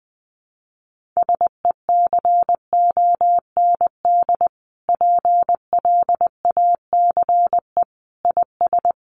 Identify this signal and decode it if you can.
Morse code, heard as a keyed tone, about 20 wpm, 700 Hz. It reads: SECOND PLACE IS
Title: Sentences from Top 1500 Words 20wpm